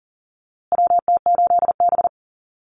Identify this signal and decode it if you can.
Morse code, heard as a keyed tone, about 40 wpm, 700 Hz. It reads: WT8B
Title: USA Call Signs Repeated 3x 40wpm